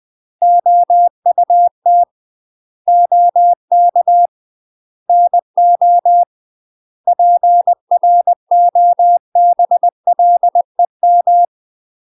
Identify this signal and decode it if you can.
Morse code, heard as a keyed tone, about 20 wpm, 700 Hz. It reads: OUT OK NO PROBLEM
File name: UK MASTER QSOs TRAINING / 20WPM / UK MASTER QSOs 0026